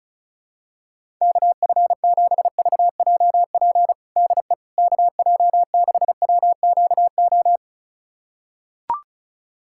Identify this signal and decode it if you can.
Morse code, heard as a keyed tone, about 34 wpm, 700 Hz. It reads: KF7VJP DE KJ6WQO
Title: USA Callsign DE Callsign 35wpm